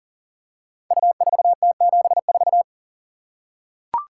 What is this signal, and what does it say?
Morse code, heard as a keyed tone, about 39 wpm, 690 Hz. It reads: U4T74
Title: Code Groups - 5 Letter:Numbers - 40wpm